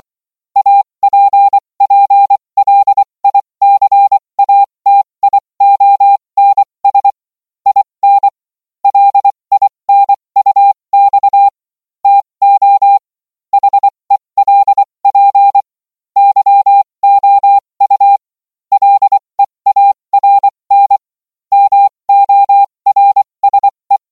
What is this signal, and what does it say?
Morse code, heard as a keyed tone, about 24 wpm, spaced as for 20 wpm, 800 Hz. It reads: APPLICATIONS IN LINUX TO HELP YOU LEARN MORSE